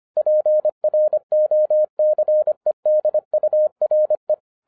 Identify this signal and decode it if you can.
Morse code, heard as a keyed tone, about 25 wpm, 610 Hz. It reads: PROCEDURE